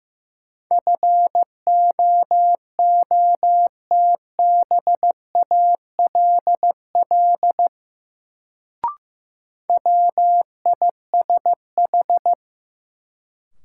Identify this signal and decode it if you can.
Morse code, heard as a keyed tone, about 16 wpm, 700 Hz. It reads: FOOTBALL WISH
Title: Sentences from Top 1600 Words - New Word Drill - 15wpm